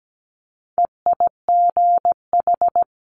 Morse code, heard as a keyed tone, about 17 wpm, 700 Hz. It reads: EIGH
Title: Multi-letter Phonemes Quick Fire 17wpm